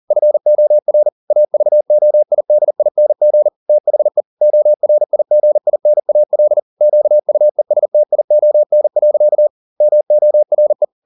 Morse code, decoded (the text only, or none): FOR AVOIDING THE ORIGINAL QUESTION. MORE